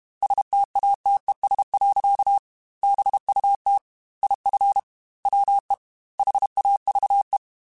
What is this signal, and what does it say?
Morse code, heard as a keyed tone, about 32 wpm, 800 Hz. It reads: STATES. BUT IF WE HAVE
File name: Technical Subjects / Technical Subjects-mp3s / AND Gate @30